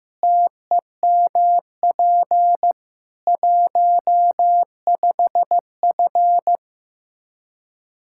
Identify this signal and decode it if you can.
Morse code, heard as a keyed tone, about 15 wpm, 700 Hz. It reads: TEMP 15F
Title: Practice QSO - Exchange 2 - Info - 15wpm